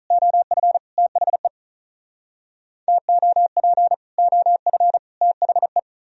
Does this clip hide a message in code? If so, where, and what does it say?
Yes: Morse code from 0.1 to 5.8 s, heard as a keyed tone, about 45 wpm, 700 Hz. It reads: OF THE TOP OF THE